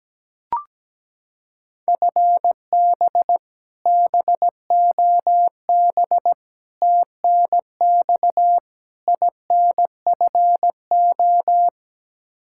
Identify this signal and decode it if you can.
Morse code, heard as a keyed tone, about 17 wpm, 700 Hz. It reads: FB BOB TNX INFO
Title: K1USN SST QSO Practice 17wpm